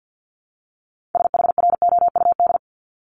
Morse code, heard as a keyed tone, about 50 wpm, 700 Hz. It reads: SHROUD
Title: Multi-letter Phonemes 50wpm